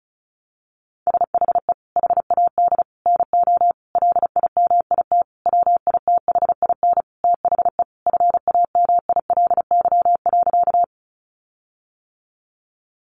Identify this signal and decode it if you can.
Morse code, heard as a keyed tone, about 34 wpm, 700 Hz. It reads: SHE HAD NO LIMIT WITHIN THE FAMILY.
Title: Sentences from Top 1500 Words 35wpm